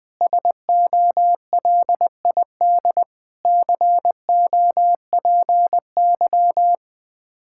Morse code, heard as a keyed tone, about 21 wpm, 700 Hz. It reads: SOLID COPY
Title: K1USN SST QSO Practice 20wpm